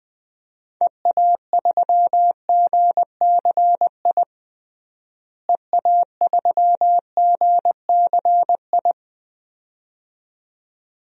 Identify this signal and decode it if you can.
Morse code, heard as a keyed tone, about 20 wpm, 690 Hz. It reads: EA3GCI EA3GCI
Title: DXCC-100 Call Signs - 6 Characters - Repeated 2x - 20wpm